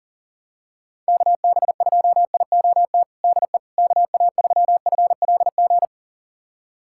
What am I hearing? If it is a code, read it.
Morse code, heard as a keyed tone, about 40 wpm, 700 Hz. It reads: KB2IOT DE KA3FLG
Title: USA Callsign DE Callsign - 6 Characters - 40wpm